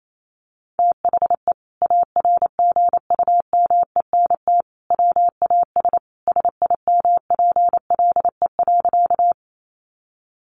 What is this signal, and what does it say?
Morse code, heard as a keyed tone, about 28 wpm, 700 Hz. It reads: THE ARGUMENT WAS SIMPLE.
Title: Sentences from Top 1400 Words 28wpm